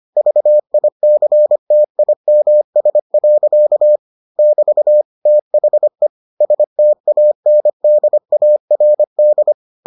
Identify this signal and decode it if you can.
Morse code, heard as a keyed tone, about 25 wpm, 600 Hz. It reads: VICTIMS. <BT> THE STANDARD